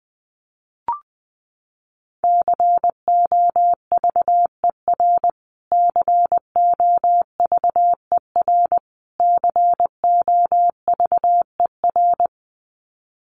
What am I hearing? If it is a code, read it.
Morse code, heard as a keyed tone, about 20 wpm, 700 Hz. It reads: COVER COVER COVER
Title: Top 500 Words Repeated 20wpm